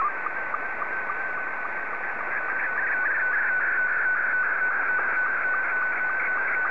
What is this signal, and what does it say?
SSTV (slow-scan television), Scottie 2.
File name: sstv / S2-VK3